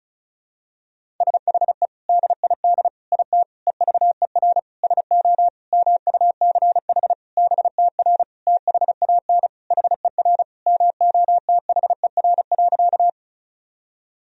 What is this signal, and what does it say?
Morse code, heard as a keyed tone, about 36 wpm, 700 Hz. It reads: SHE DID IT EVER SO MUCH BTR THAN HER MOTHER.
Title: Sentences from Top 400 Words and Abbreviations 35wpm